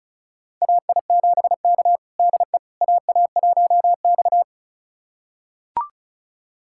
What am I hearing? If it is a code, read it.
Morse code, heard as a keyed tone, about 36 wpm, 700 Hz. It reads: AI7K DE AA1X
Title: USA Callsign DE Callsign - 4 Characters - 35wpm